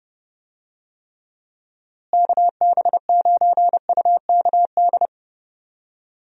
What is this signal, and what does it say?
Morse code, heard as a keyed tone, about 30 wpm, 700 Hz. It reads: KB9UKD
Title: USA Call Signs - 6 Characters - Repeated 3x - 30wpm